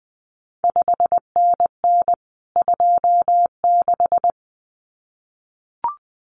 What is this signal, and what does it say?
Morse code, heard as a keyed tone, about 19 wpm, 700 Hz. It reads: 5NN 26
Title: CQ WW DX Contest Exchange Encore - 20wpm